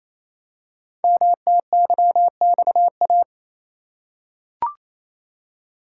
Morse code, heard as a keyed tone, about 29 wpm, 700 Hz. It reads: MTYXA
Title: Code Groups - 5 Letters - 28wpm